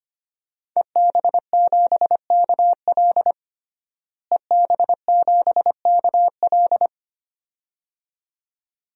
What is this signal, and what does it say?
Morse code, heard as a keyed tone, about 25 wpm, 700 Hz. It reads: EB7KL EB7KL
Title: DXCC-100 Call Signs - 5 Characters - Repeated 2x - 25wpm